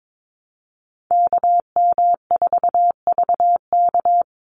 Morse code, heard as a keyed tone, about 22 wpm, 700 Hz. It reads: KM4VK
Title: USA Call Signs 22wpm